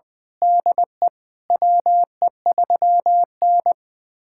Morse code, heard as a keyed tone, about 20 wpm, 700 Hz. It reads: DE WE3N
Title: USA Callsign DE Callsign - 4 Characters - 20wpm